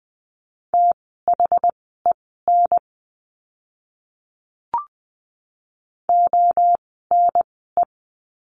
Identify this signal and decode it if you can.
Morse code, heard as a keyed tone, about 20 wpm, spaced as for 15 wpm, 700 Hz. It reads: THEN ONE
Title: 068 Lesson 12f - Letter d - Words 20wpm